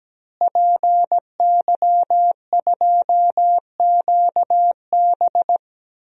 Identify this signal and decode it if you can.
Morse code, heard as a keyed tone, about 17 wpm, 700 Hz. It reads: PY2QB
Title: DXCC-100 Call Signs - 5 Characters - Repeated 2x - 17wpm